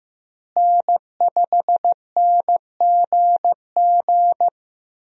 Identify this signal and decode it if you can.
Morse code, heard as a keyed tone, about 15 wpm, 700 Hz. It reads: N5NGG
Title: USA Call Signs - 5 Characters - Repeated 3x - 15wpm